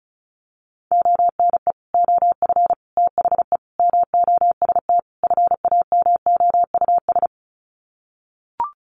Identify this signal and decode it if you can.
Morse code, heard as a keyed tone, about 34 wpm, 700 Hz. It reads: ONE OF THE MOST FAMOUS
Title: Sets of 5 Words Encore2 35wpm